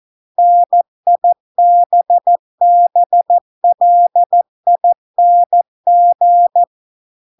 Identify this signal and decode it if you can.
Morse code, heard as a keyed tone, about 14 wpm, 700 Hz. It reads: NIBBLING